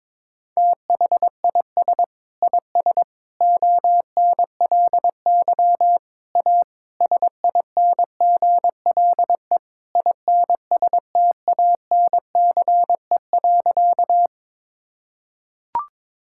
Morse code, heard as a keyed tone, about 22 wpm, 700 Hz. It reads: THIS IS ONLY A SINGLE INSTANCE.
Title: Sentences from Top 1000 Words - Review Part 5 - 22wpm